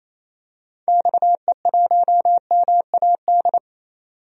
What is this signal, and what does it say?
Morse code, heard as a keyed tone, about 28 wpm, 700 Hz. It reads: XE1MAD
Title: DXCC-100 Call Signs - 6 Characters - 28wpm